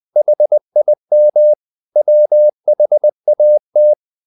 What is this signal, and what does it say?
Morse code, heard as a keyed tone, about 20 wpm, 600 Hz. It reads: HIM WHAT
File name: the old man and the sea / the old man and the sea part 1 20wpm